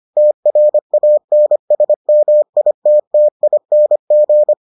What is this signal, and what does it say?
Morse code, heard as a keyed tone, about 25 wpm, 600 Hz. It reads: TRANSMITTING